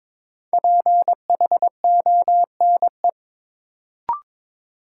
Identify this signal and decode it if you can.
Morse code, heard as a keyed tone, about 21 wpm, 700 Hz. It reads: PHONE
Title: Sentences from Top 1100 Words - New Word Drill - 22wpm